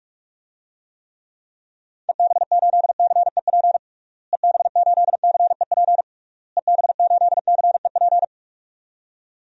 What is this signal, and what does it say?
Morse code, heard as a keyed tone, about 46 wpm, 700 Hz. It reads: EB8CEP EB8CEP EB8CEP
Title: DXCC-100 Call Signs - 6 Characters - Repeated 3x - 45wpm